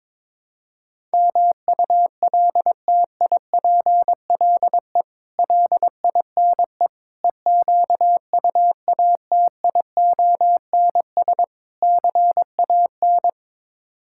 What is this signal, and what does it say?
Morse code, heard as a keyed tone, about 22 wpm, 700 Hz. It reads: MULTIPLE LINE EQUATIONS CAN
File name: practice / 22wpm / Build A Phrase - 5-Word Phrases - 22wpm